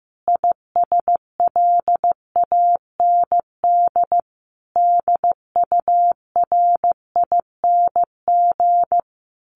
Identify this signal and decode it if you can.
Morse code, heard as a keyed tone, about 15 wpm, 700 Hz. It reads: ISLAND DURING
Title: Sentences from Top 1400 Words 15wpm